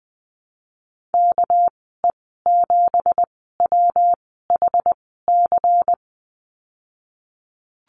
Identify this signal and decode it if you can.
Morse code, heard as a keyed tone, about 19 wpm, spaced as for 14 wpm, 700 Hz. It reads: KE7WHC
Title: 200 Lesson 32g - Letter k - Call Signs 20wpm